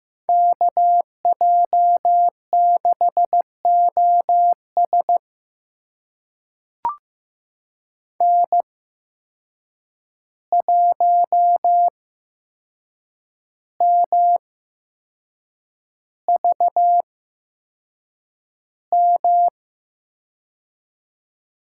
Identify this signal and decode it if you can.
Morse code, heard as a keyed tone, about 15 wpm, spaced as for 6 wpm, 700 Hz. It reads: KJ6OS N1MVM
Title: USA Call Signs - 5 Characters - 8x Character Spacing - 15wpm